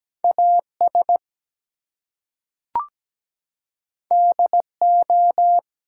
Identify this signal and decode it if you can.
Morse code, heard as a keyed tone, about 17 wpm, 700 Hz. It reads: AS DO
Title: Two Letter Words - 17wpm